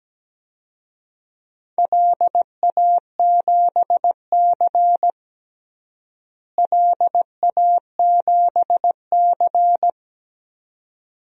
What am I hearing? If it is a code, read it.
Morse code, heard as a keyed tone, about 17 wpm, 700 Hz. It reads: LA7C LA7C
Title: DXCC-100 Call Signs - 4 Characters - Repeated 2x - 17wpm